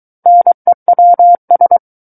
Morse code, heard as a keyed tone, about 23 wpm, 700 Hz. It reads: NEWS
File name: Book One / 14-Chapter 14 - 23 WPM